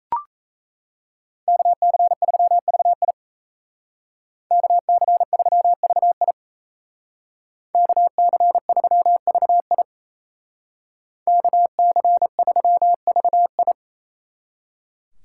Morse code, heard as a keyed tone, about 33 wpm, 700 Hz. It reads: KC3VI KC3VI KC3VI KC3VI
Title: Speed Racing - USA Call Signs 28wpm